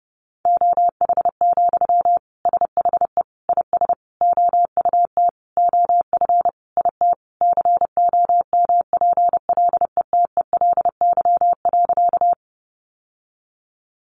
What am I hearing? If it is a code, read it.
Morse code, heard as a keyed tone, about 30 wpm, 700 Hz. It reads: OH, SHE IS OUT OF IT COMPLETELY.
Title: Sentences from Top 1100 Words 30wpm